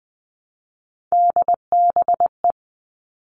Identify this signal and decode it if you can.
Morse code, heard as a keyed tone, about 20 wpm, 700 Hz. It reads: DBE
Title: All Cut Numbers - 3-Digit 20wpm